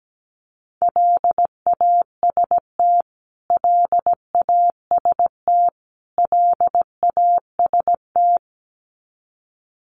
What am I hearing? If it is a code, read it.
Morse code, heard as a keyed tone, about 17 wpm, 700 Hz. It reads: LAST LAST LAST
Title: Top 500 Words Repeated 17wpm